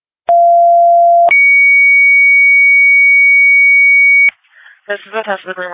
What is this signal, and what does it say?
Two-tone page, heard as a one-second tone, then a three-second tone, about 690 then 2250 Hz, with speech. It is heard from 0.3 to 4.3 s.